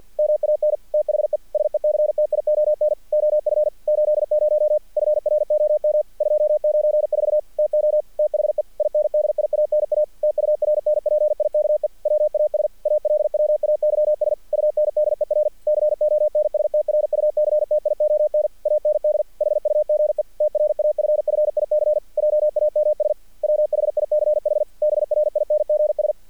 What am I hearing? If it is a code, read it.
Morse code, heard as a keyed tone, about 48 wpm, 600 Hz. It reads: MAN THE SECTION OF 80 FROM 294 TO THE INDIANA TURNPIKE WAS ALWAYS UNDER CONSTRUCTION AND HUGE TRAFFIC JAMS WHICH BRINGS
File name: archived / CRSnet